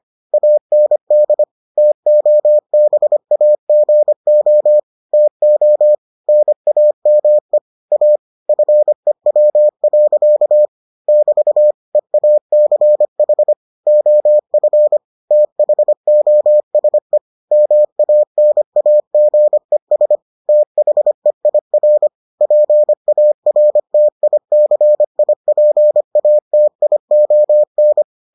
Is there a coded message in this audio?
Yes: Morse code from 0.3 to 28.0 s, heard as a keyed tone, about 25 wpm, 600 Hz. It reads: AND TOBAGO TO NAME A FEW. <BT> EACH OF THOSE MANAGES THEIR PARTICIPATION